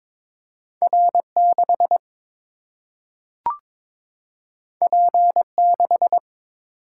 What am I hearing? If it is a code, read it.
Morse code, heard as a keyed tone, about 22 wpm, 700 Hz. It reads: R 6 P 6
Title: Single-Letter Single-Number 22wpm